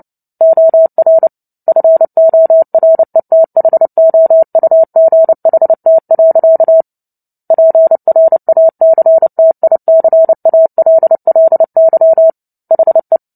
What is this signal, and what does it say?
Morse code, heard as a keyed tone, about 30 wpm, 650 Hz. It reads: OR FORETHOUGHT. PRACTICALLY HE
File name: Book Two / 03-Chapter 3 - 29 WPM